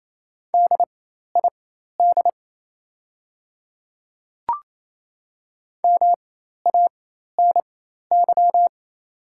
Morse code, heard as a keyed tone, about 28 wpm, spaced as for 13 wpm, 700 Hz. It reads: DID MANY
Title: 4x Character Spacing - Top 200 Words - 28wpm